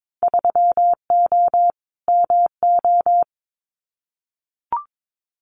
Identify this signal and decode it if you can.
Morse code, heard as a keyed tone, about 22 wpm, 700 Hz. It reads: 3O MO
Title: Winter Field Day Exchanges Encore 22wpm